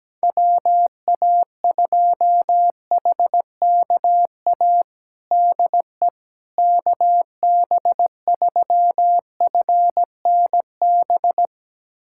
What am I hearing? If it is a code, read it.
Morse code, heard as a keyed tone, about 17 wpm, 700 Hz. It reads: WA2HKA DE KB3FNB
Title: USA Callsign DE Callsign - 6 Characters - 17wpm